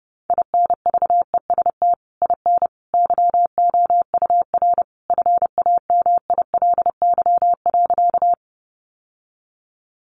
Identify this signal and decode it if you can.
Morse code, heard as a keyed tone, about 30 wpm, 700 Hz. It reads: INVEST IN YOUR FAMILY.
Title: Sentences from Top 1100 Words 30wpm